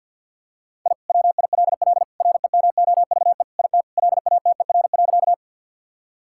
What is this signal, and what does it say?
Morse code, heard as a keyed tone, about 50 wpm, 700 Hz. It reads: I WILL REMOVE IT LATER.
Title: Sentences from Top 1000 Words - Review Part 5 - 50wpm